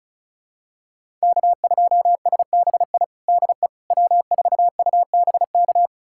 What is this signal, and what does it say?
Morse code, heard as a keyed tone, about 36 wpm, 690 Hz. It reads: K2SBI DE W4UBK
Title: USA Callsign DE Callsign - 5 Characters - 35wpm